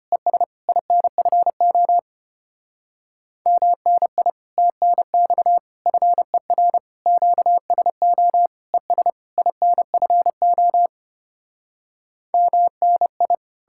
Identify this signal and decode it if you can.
Morse code, heard as a keyed tone, about 33 wpm, 700 Hz. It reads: ES INFO MNI TNX FER QSO ES INFO MNI